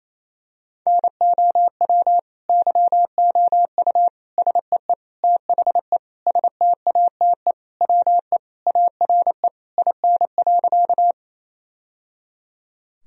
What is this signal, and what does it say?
Morse code, heard as a keyed tone, about 28 wpm, 700 Hz. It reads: NOW YOU SEE THE STATE WE ARE IN.
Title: Sentences from Top 1000 Words - Review Part 2 - 28wpm